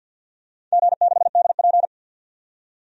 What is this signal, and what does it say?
Morse code, heard as a keyed tone, about 51 wpm, 700 Hz. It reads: G6DP
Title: DXCC-100 Call Signs - 4 Characters - 50wpm